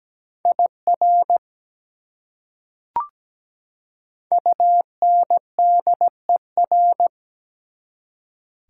Morse code, heard as a keyed tone, about 17 wpm, 700 Hz. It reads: IR UNDER
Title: Prefixes and Suffixes 17wpm